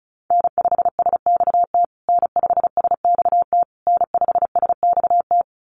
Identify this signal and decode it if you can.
Morse code, heard as a keyed tone, about 34 wpm, 700 Hz. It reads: N5SXT N5SXT N5SXT
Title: USA Call Signs Repeated 3x 35wpm